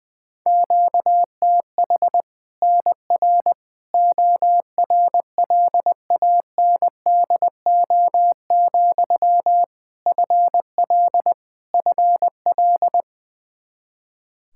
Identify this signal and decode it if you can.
Morse code, heard as a keyed tone, about 21 wpm, 700 Hz. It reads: QTH NR ORLANDO, FL FL
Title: Practice QSO - Exchange 1 - Beginning - 20wpm